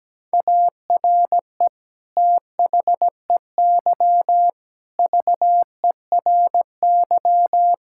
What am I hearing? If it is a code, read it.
Morse code, heard as a keyed tone, about 17 wpm, 700 Hz. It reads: ARE THEY VERY
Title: Sentences from Top 1000 Words - Review Part 5 - 17wpm